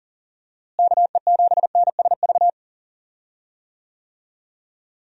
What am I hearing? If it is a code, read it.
Morse code, heard as a keyed tone, about 40 wpm, 700 Hz. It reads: KE7NSV
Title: USA Call Signs - 6 Characters - Repeated 2x - 40wpm